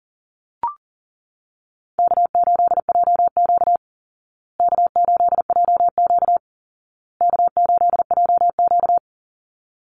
Morse code, heard as a keyed tone, about 40 wpm, 700 Hz. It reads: K8JQ K8JQ K8JQ
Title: Easy Call Signs Repeated 3x - 40wpm